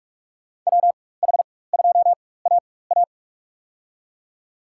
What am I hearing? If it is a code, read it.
Morse code, heard as a keyed tone, about 46 wpm, spaced as for 21 wpm, 700 Hz. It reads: WH2AA
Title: USA Call Signs - 5 Characters - 4x Character Spacing - 45wpm